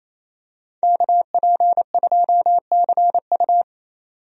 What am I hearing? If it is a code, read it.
Morse code, heard as a keyed tone, about 28 wpm, 700 Hz. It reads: KP2CU
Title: DXCC-100 Call Signs - 5 Characters - 28wpm